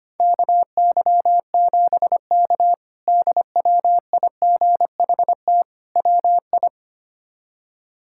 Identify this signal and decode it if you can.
Morse code, heard as a keyed tone, about 25 wpm, 700 Hz. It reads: KY7K DWIGHT WI
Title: K1USN SST QSO Practice - 25wpm